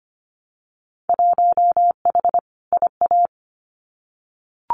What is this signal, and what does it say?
Morse code, heard as a keyed tone, about 25 wpm, 700 Hz. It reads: 1H IA
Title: Winter Field Day Exchanges 25wpm